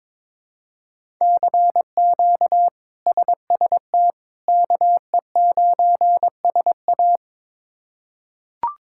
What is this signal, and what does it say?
Morse code, heard as a keyed tone, about 22 wpm, 700 Hz. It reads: CQ SST KE9SA
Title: K1USN SST QSO Practice - 22wpm